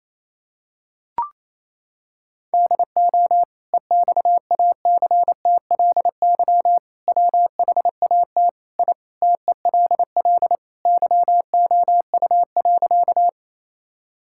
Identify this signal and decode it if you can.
Morse code, heard as a keyed tone, about 28 wpm, 700 Hz. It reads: DO EXACTLY WHAT I TELL YOU.
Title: Sentences from Top 1000 Words - Review Part 6 - 28wpm